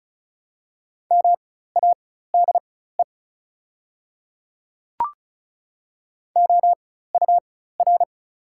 Morse code, heard as a keyed tone, about 34 wpm, spaced as for 16 wpm, 690 Hz. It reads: MADE OUR
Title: 4x Character Spacing - Top 500 Words - 35wpm